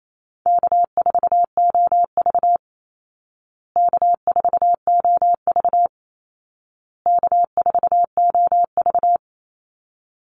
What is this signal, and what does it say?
Morse code, heard as a keyed tone, about 28 wpm, 700 Hz. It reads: K4OV K4OV K4OV
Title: Easy Call Signs Repeated 3x - 28wpm